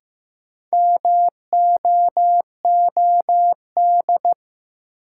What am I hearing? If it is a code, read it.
Morse code, heard as a keyed tone, about 15 wpm, 700 Hz. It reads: MOOD
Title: Sentences from Top 1100 Words - New Word Drill - 15wpm